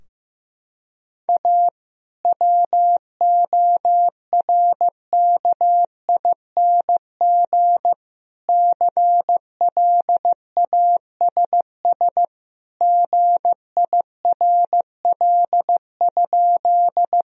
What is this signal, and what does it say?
Morse code, heard as a keyed tone, about 15 wpm, 700 Hz. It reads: A WORKING CLASS GIRL?
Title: Sentences from Top 1000 Words - Review Part 4 - 15wpm